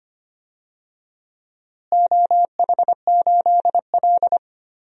Morse code, heard as a keyed tone, about 25 wpm, 700 Hz. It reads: OH8L
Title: DXCC-100 Call Signs - 4 Characters - Repeated 2x - 25wpm